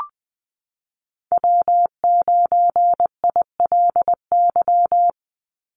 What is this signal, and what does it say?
Morse code, heard as a keyed tone, about 20 wpm, 700 Hz. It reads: W9ILY
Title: Call Signs - 20wpm